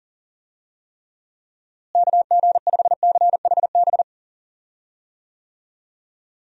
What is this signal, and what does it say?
Morse code, heard as a keyed tone, about 40 wpm, 700 Hz. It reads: KG5CHB
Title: USA Call Signs - 6 Characters - Repeated 2x - 40wpm